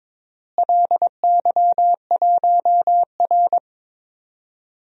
Morse code, heard as a keyed tone, about 21 wpm, 700 Hz. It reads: LY1R
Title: DXCC-100 Call Signs - 4 Characters - Repeated 2x - 22wpm